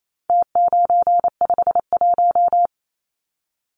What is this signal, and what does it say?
Morse code, heard as a keyed tone, about 28 wpm, 700 Hz. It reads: T951
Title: Numbers - Contest Style - 4-Digit 28wpm